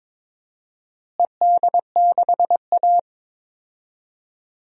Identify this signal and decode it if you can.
Morse code, heard as a keyed tone, about 21 wpm, 700 Hz. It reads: ED6A
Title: DXCC-100 Call Signs - 4 Characters - Repeated 2x - 22wpm